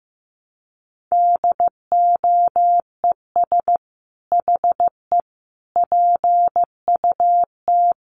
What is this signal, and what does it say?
Morse code, heard as a keyed tone, about 15 wpm, 700 Hz. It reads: DOES HE PUT
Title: Sentences from Top 1300 Words 15wpm